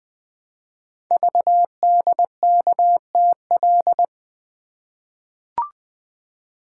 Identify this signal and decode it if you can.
Morse code, heard as a keyed tone, about 20 wpm, 700 Hz. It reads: VDKTL
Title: Code Groups - 5 Letters - 20wpm